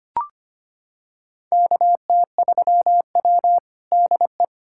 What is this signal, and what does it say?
Morse code, heard as a keyed tone, about 25 wpm, 700 Hz. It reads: KT3W DE
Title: USA Callsign DE Callsign - 4 Characters - 25wpm